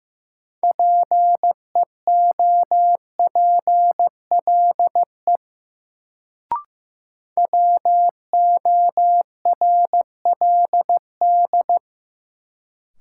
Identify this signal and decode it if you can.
Morse code, heard as a keyed tone, about 15 wpm, 700 Hz. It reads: PEOPLE WORLD
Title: Sentences from Top 200 Words - New Word Drill - 15wpm